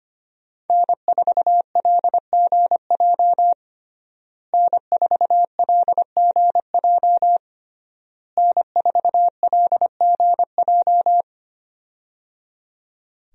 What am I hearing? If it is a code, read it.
Morse code, heard as a keyed tone, about 25 wpm, 700 Hz. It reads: N4LGJ N4LGJ N4LGJ
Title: USA Call Signs - 5 Characters - Repeated 3x - 25wpm